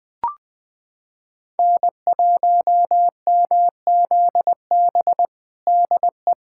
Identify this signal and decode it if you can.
Morse code, heard as a keyed tone, about 20 wpm, 700 Hz. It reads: N1MZB DE
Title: USA Callsign DE Callsign - 5 Characters - 20wpm